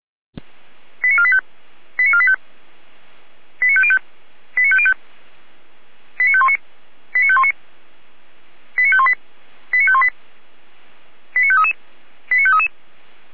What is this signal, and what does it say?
Five-tone selective call (ZVEI), heard as quick runs of five tones, tones of about 70 ms. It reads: 89485 89485 89505 89505 89519 89519 89518 89518 89533 89533